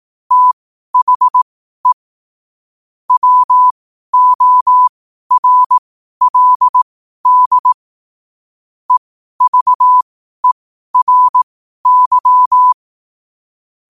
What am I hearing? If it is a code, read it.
Morse code, heard as a keyed tone, about 18 wpm, spaced as for 12 wpm, 1000 Hz. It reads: THE WORLD EVERY
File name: qotdfiles / 12wpm